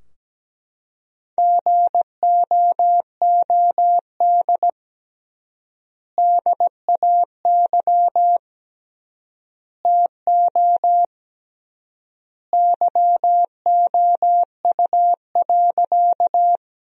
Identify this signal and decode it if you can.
Morse code, heard as a keyed tone, about 17 wpm, 700 Hz. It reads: GOOD DAY TO YOU.
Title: 3x Word-Spacing - Sentences from Top 200 Words - 17wpm